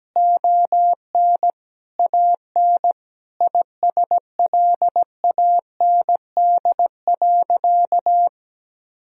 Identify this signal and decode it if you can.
Morse code, heard as a keyed tone, about 17 wpm, 700 Hz. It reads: ON AN ISLAND.
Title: Sentences from Top 1000 Words - Review Part 3 - 17wpm